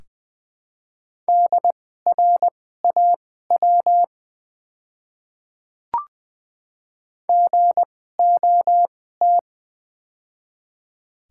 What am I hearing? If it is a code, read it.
Morse code, heard as a keyed tone, about 20 wpm, spaced as for 14 wpm, 700 Hz. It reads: DRAW GOT
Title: 2x Character Spacing - Top 300 Words - 20wpm